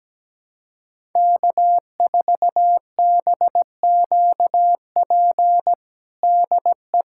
Morse code, heard as a keyed tone, about 17 wpm, 700 Hz. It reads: K4BQP DE
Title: USA Callsign DE Callsign 17wpm